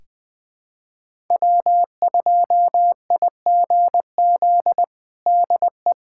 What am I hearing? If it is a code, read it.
Morse code, heard as a keyed tone, about 20 wpm, 700 Hz. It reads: W2IGZ DE
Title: USA Callsign DE Callsign - 5 Characters - 20wpm